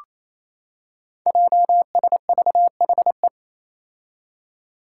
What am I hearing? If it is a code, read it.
Morse code, heard as a keyed tone, about 28 wpm, 700 Hz. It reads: JSVHE
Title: Five Challenging Letter-Number Combinations 28wpm